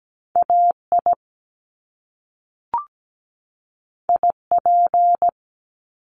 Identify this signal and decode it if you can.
Morse code, heard as a keyed tone, about 17 wpm, 700 Hz. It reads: AI IP
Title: Top Two Letter Combinations in Words 17wpm